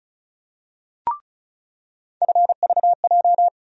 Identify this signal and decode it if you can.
Morse code, heard as a keyed tone, about 34 wpm, 700 Hz. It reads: FVJ
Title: USA Call Sign Postfixes - 3 Letters 35wpm